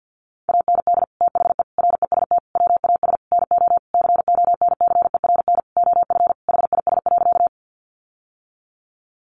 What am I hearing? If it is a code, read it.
Morse code, heard as a keyed tone, about 50 wpm, 700 Hz. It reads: AND THE REST WAS NO CONCERN OF HIS.
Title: Sentences from Top 1000 Words 50wpm